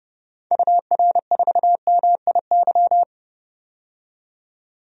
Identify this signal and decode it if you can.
Morse code, heard as a keyed tone, about 30 wpm, 700 Hz. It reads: UR4MIY
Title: DXCC-100 Call Signs - 6 Characters - 30wpm